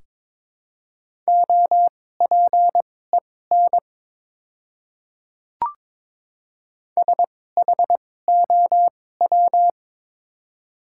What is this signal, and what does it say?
Morse code, heard as a keyed tone, about 21 wpm, spaced as for 16 wpm, 700 Hz. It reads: OPEN SHOW
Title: 2x Character Spacing - Top 300 Words - 22wpm